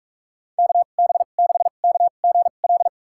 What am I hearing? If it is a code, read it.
Morse code, heard as a keyed tone, about 46 wpm, spaced as for 33 wpm, 700 Hz. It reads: KB6KGL
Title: USA Call Signs - 6 Characters - 2x Character Spacing - 45wpm